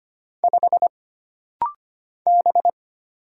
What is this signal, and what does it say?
Morse code, heard as a keyed tone, about 25 wpm, 700 Hz. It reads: 5 B
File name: Single Character (Letter:Number:Punctuation:Prosign) - Rapid-Fire - 25wpm